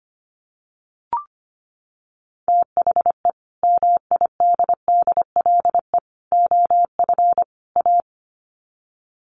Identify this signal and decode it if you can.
Morse code, heard as a keyed tone, about 25 wpm, 700 Hz. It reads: THE MIDDLE OF A
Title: Sets of 4 Words 25wpm